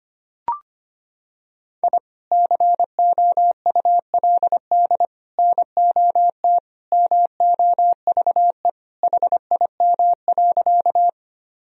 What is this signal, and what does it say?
Morse code, heard as a keyed tone, about 25 wpm, 700 Hz. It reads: I COULD NOT MOVE HIM.
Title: Sentences from Top 1000 Words - Review Part 1 - 25wpm